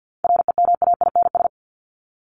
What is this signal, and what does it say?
Morse code, heard as a keyed tone, about 50 wpm, 700 Hz. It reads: REMAINS
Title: ES:S Suffix - Top 100 Words with Base Words 50wpm